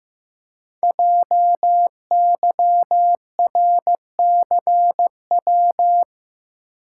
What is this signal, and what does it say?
Morse code, heard as a keyed tone, about 15 wpm, 700 Hz. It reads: JYRCW
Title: Code Groups - 5 Letter:Numbers - 15wpm